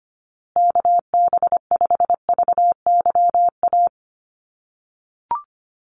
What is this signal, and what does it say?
Morse code, heard as a keyed tone, about 25 wpm, 700 Hz. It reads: KB5VYA
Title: USA Call Signs Repeated 2x 25wpm